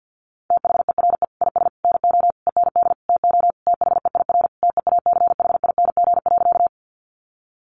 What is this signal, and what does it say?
Morse code, heard as a keyed tone, about 50 wpm, 700 Hz. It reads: THERE IS NO END TO THEIR TEACHING.
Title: Sentences from Top 1100 Words 50wpm